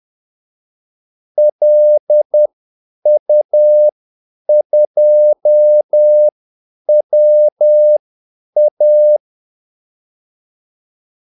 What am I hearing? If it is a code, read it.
Morse code, heard as a keyed tone, about 10 wpm, spaced as for 8 wpm, 600 Hz. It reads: LU2WA
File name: Callsigns-10-8wpm